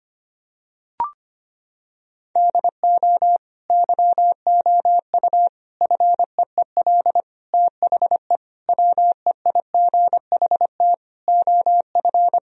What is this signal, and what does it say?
Morse code, heard as a keyed tone, about 25 wpm, 700 Hz. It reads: DO YOU FEEL THE WEIGHT OF
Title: Sentences from Top 1000 Words - Review Part 3 - 25wpm